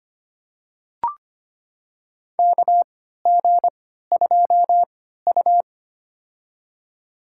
Morse code, heard as a keyed tone, about 25 wpm, spaced as for 14 wpm, 700 Hz. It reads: KG2U
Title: USA Call Signs - 4 Characters - 3x Character Spacing - 25wpm